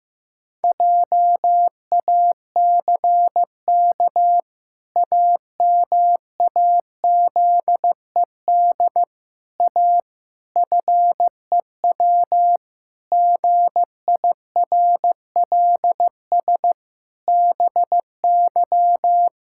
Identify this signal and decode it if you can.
Morse code, heard as a keyed tone, about 15 wpm, 700 Hz. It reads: JACK AMAZED A FEW GIRLS BY
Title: Pangram Challenge v2 - 15wpm